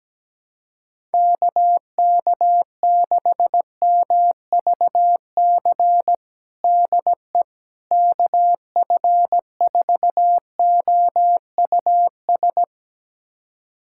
Morse code, heard as a keyed tone, about 17 wpm, 700 Hz. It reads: KK6MVC DE KF4OUS
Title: USA Callsign DE Callsign - 6 Characters - 17wpm